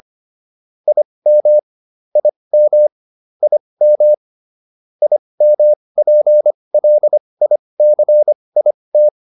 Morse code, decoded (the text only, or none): IM IM IM IMPLICIT